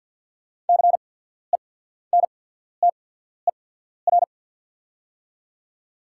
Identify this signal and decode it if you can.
Morse code, heard as a keyed tone, about 48 wpm, spaced as for 14 wpm, 690 Hz. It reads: CENTER
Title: 8x Character Spacing - Top 500 Words - 50wpm